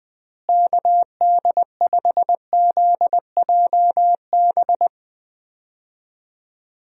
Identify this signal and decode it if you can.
Morse code, heard as a keyed tone, about 20 wpm, 700 Hz. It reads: KD5ZJB
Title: USA Call Signs - 6 Characters - Repeated 3x - 20wpm